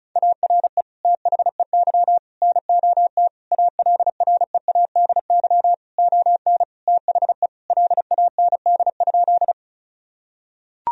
Morse code, heard as a keyed tone, about 36 wpm, 700 Hz. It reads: ARE THEY NOT ALREADY ON THE LAND?
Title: Sentences from Top 1000 Words - Review Part 1 - 35wpm